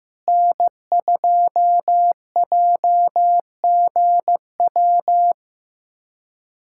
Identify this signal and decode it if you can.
Morse code, heard as a keyed tone, about 15 wpm, 700 Hz. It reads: N2JGW
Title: USA Callsign DE Callsign - 5 Characters - 15wpm